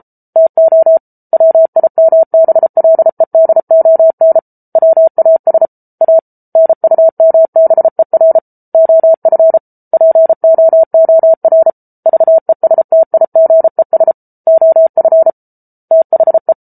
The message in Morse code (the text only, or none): TO WIMBLEDON WAS A NUMBER OF POOR VESTIGES OF THE